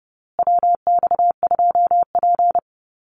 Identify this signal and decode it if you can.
Morse code, heard as a keyed tone, about 30 wpm, 700 Hz. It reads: WX2P
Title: USA Call Signs Repeated 3x 30wpm